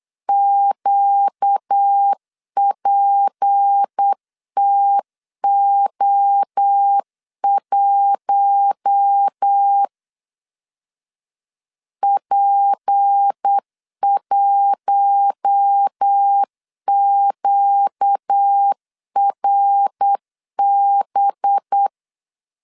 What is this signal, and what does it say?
Morse code, heard as a keyed tone, about 8 wpm, 800 Hz. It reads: QPTO1 P1QRB
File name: LEZIONI / 6 PKX0 / 6 P 8wpm 3025